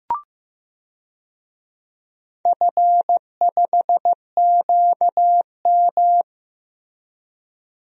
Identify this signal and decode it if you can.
Morse code, heard as a keyed tone, about 15 wpm, 700 Hz. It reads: F5QM
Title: DXCC-100 Call Signs - 4 Characters - Repeated 2x - 15wpm